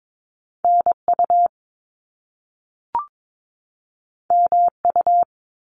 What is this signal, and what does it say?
Morse code, heard as a keyed tone, about 22 wpm, 700 Hz. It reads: NU MU
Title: Top Two Letter Combinations in Words 22wpm